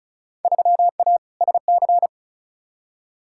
Morse code, heard as a keyed tone, about 36 wpm, 700 Hz. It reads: 3A SC
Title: Field Day Exchanges Encore - 35wpm